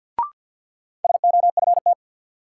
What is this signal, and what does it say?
Morse code, heard as a keyed tone, about 48 wpm, 700 Hz. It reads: SOFT
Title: Sentences from Top 1500 Words - New Word Drill - 50wpm